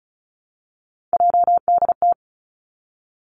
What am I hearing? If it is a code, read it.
Morse code, heard as a keyed tone, about 36 wpm, 700 Hz. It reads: JDT
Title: Three Challenging Letter-Number Combinations 35wpm